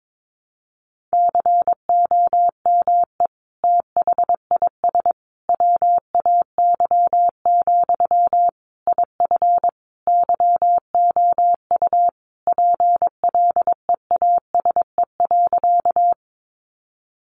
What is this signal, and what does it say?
Morse code, heard as a keyed tone, about 22 wpm, 700 Hz. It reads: COME THIS WAY, IF YOU PLEASE.
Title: Sentences from Top 1500 Words 22wpm